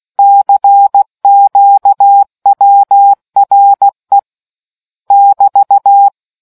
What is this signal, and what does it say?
Morse code, heard as a keyed tone, about 16 wpm, 800 Hz. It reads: CQWRE <BT>